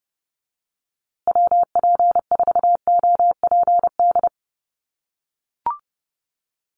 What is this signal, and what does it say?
Morse code, heard as a keyed tone, about 30 wpm, 700 Hz. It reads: WP4OPD
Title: USA Call Signs Repeated 2x 30wpm